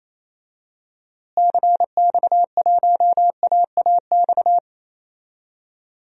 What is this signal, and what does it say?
Morse code, heard as a keyed tone, about 28 wpm, 700 Hz. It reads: CX1AAX
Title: DXCC-100 Call Signs - 6 Characters - Repeated 3x - 28wpm